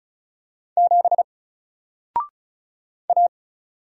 Morse code, heard as a keyed tone, about 36 wpm, 700 Hz. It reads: 7 A
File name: Single-Letter-Number-35wpm